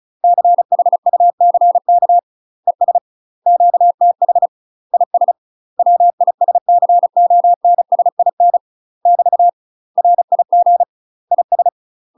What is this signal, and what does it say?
Morse code, heard as a keyed tone, about 34 wpm, 700 Hz. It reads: CHUCK ES QTH IS WISCONSIN <BT> RIG IS
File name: UK MASTER QSOs TRAINING / 35WPM / UK MASTER QSOs 0025